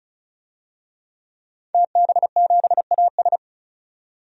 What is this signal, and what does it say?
Morse code, heard as a keyed tone, about 36 wpm, 700 Hz. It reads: TB7AS
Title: DXCC-100 Call Signs - 5 Characters - 35wpm